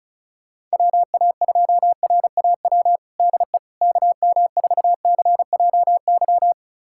Morse code, heard as a keyed tone, about 36 wpm, 700 Hz. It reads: WA2RAW DE KM4CJY
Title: USA Callsign DE Callsign - 6 Characters - 35wpm